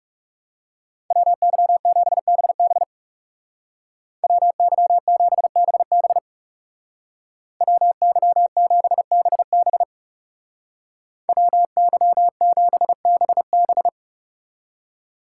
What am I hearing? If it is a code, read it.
Morse code, heard as a keyed tone, about 37 wpm, 700 Hz. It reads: WY7BB WY7BB WY7BB WY7BB
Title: Speed Racing - USA 2x2 Call Signs 30wpm